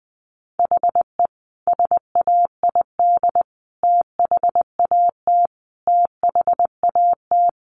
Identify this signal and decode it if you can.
Morse code, heard as a keyed tone, about 20 wpm, 700 Hz. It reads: HE SAID THAT THAT
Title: Sentences from Top 200 Words 20wpm